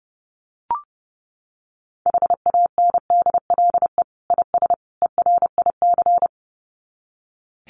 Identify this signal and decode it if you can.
Morse code, heard as a keyed tone, about 30 wpm, 700 Hz. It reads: HANDLE IS ERIC
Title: Common QSO Phrases 30wpm